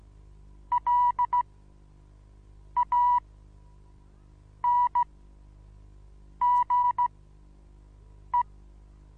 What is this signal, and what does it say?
Morse code, heard as a keyed tone, about 15 wpm, spaced as for 5 wpm, 980 Hz. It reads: LANGE
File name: Onlinelehrgang / morsen / DJ4UF Morsen25